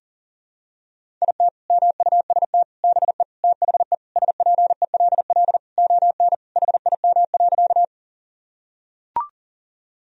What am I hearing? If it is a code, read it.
Morse code, heard as a keyed tone, about 40 wpm, 700 Hz. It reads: IT MUST BE THE SPELL ON HIM.
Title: Sentences from Top 1000 Words - Review Part 1 - 40wpm